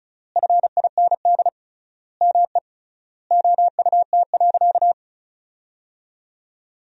Morse code, heard as a keyed tone, about 34 wpm, 700 Hz. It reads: FIND ME OUT.
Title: 3x Word-Spacing - Sentences from Top 500 Words - 35wpm